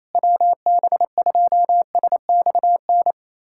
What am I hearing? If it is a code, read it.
Morse code, heard as a keyed tone, about 28 wpm, 700 Hz. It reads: WB2SXN
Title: USA Callsign DE Callsign 28wpm